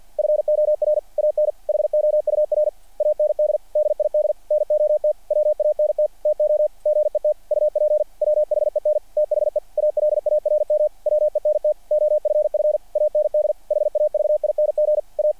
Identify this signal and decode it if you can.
Morse code, heard as a keyed tone, about 48 wpm, 590 Hz. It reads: FOR AN HOUR AND DID NOT WANT TO GET UP WHEN THE ALARM WENT OFF AND HAVING A
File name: archived / CRSnet